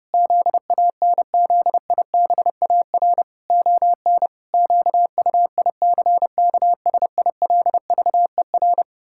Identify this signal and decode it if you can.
Morse code, heard as a keyed tone, about 30 wpm, 700 Hz. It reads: ZANZIBAR ON QUICKSILVER
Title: Pangram Challenge v2 - 30wpm